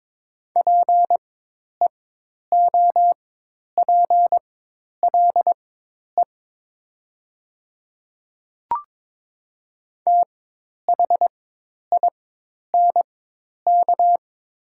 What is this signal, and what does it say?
Morse code, heard as a keyed tone, about 22 wpm, spaced as for 10 wpm, 700 Hz. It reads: PEOPLE THINK
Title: 4x Character Spacing - Top 200 Words - 22wpm